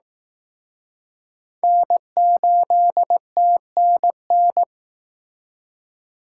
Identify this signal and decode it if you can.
Morse code, heard as a keyed tone, about 19 wpm, 700 Hz. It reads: N8TNN